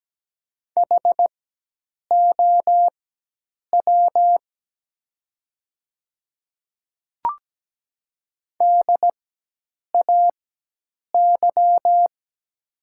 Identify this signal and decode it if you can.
Morse code, heard as a keyed tone, about 17 wpm, spaced as for 8 wpm, 700 Hz. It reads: HOW DAY
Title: 4x Character Spacing - Top 100 Words - 17wpm